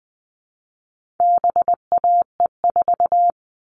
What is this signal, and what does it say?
Morse code, heard as a keyed tone, about 20 wpm, 700 Hz. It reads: BAE4
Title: All Cut Numbers - 4-Digit 20wpm